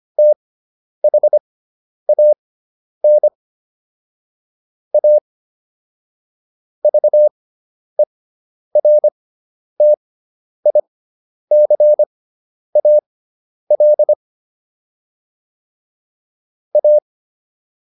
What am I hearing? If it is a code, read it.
Morse code, heard as a keyed tone, about 25 wpm, spaced as for 10 wpm, 600 Hz. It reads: THAN A VERTICAL A